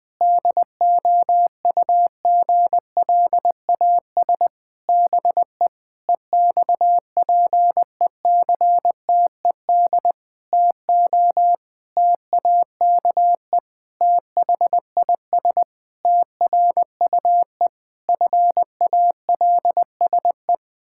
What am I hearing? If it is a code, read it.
Morse code, heard as a keyed tone, about 20 wpm, 700 Hz. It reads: DOUGLAS BE EXPECTED TO TAKE THIS TRUE FALSE
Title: Pangram Challenge v2 - 20wpm